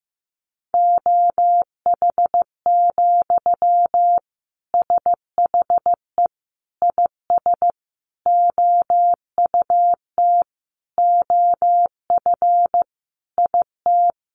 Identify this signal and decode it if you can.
Morse code, heard as a keyed tone, about 15 wpm, 700 Hz. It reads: OH, SHE IS OUT OF IT
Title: Sentences from Top 1100 Words 15wpm